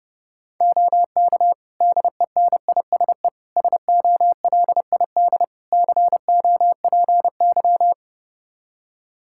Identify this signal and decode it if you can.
Morse code, heard as a keyed tone, about 30 wpm, 700 Hz. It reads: OK DENISE SOLID COPY
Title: K1USN SST QSO Practice 30wpm